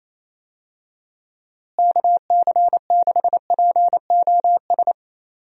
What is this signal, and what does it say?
Morse code, heard as a keyed tone, about 28 wpm, 700 Hz. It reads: KC6POS
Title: USA Call Signs - 6 Characters - Repeated 2x - 28wpm